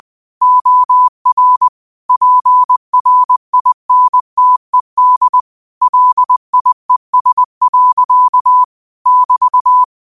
Morse code, heard as a keyed tone, about 20 wpm, 1000 Hz. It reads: OR PRINTED LIES. <BT>
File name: qotdfiles / 20wpm